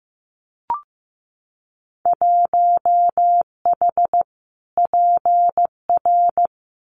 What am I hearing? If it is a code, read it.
Morse code, heard as a keyed tone, about 15 wpm, 700 Hz. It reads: 1H PR
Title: Winter Field Day Exchanges Encore 15wpm